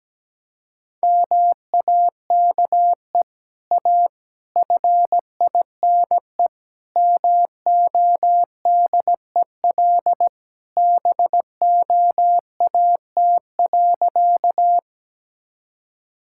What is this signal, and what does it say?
Morse code, heard as a keyed tone, about 17 wpm, 700 Hz. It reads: MAKE A FINE MODEL BOAT.
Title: Sentences from Top 1000 Words - Review Part 5 - 17wpm